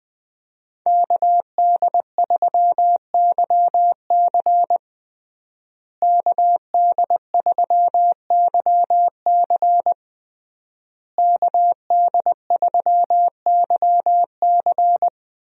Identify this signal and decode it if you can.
Morse code, heard as a keyed tone, about 20 wpm, 700 Hz. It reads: KD3YC KD3YC KD3YC
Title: USA Call Signs - 5 Characters - Repeated 3x - 20wpm